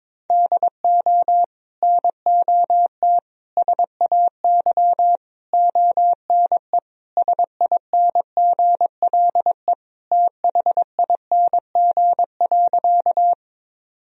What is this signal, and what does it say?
Morse code, heard as a keyed tone, about 22 wpm, 700 Hz. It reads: DO NOT SAY ONE SINGLE THING.
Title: Sentences from Top 1000 Words - Review Part 4 - 22wpm